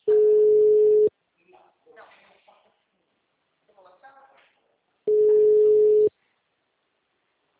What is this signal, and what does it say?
Ringback tone (Europe, 425 Hz), with speech.
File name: out- -2010- - - .45260